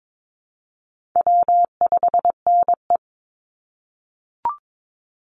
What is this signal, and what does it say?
Morse code, heard as a keyed tone, about 22 wpm, 700 Hz. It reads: W5NE
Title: Call Signs - 22wpm